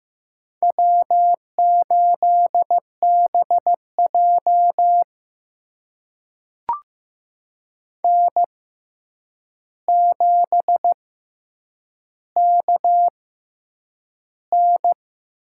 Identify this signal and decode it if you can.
Morse code, heard as a keyed tone, about 15 wpm, spaced as for 8 wpm, 700 Hz. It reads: W8BJ N7KN
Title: USA Call Signs - 4 Characters - 6x Character Spacing - 15wpm